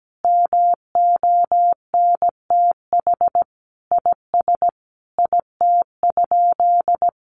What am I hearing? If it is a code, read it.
Morse code, heard as a keyed tone, about 17 wpm, 700 Hz. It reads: MONTH IS IT?
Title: Sentences from Top 800 Words 17wpm